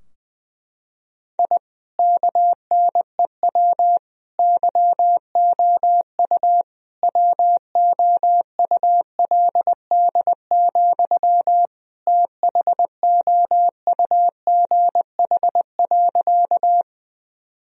Morse code, heard as a keyed tone, about 20 wpm, 700 Hz. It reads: I KNEW YOU WOULD, THOUGH.
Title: Sentences from Top 1000 Words - Review Part 2 - 20wpm